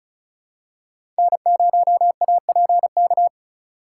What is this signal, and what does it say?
Morse code, heard as a keyed tone, about 36 wpm, 700 Hz. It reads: N0APK
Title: USA Call Signs - 5 Characters - 4x Character Spacing - 35wpm